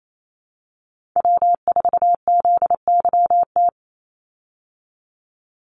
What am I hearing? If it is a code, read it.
Morse code, heard as a keyed tone, about 28 wpm, 690 Hz. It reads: W4ZYT
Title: Call Signs Repeated 3x - 28wpm